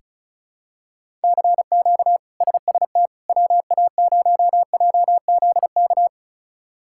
Morse code, heard as a keyed tone, about 36 wpm, 700 Hz. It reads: CQ SST WA0JZK
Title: K1USN SST QSO Practice - 35wpm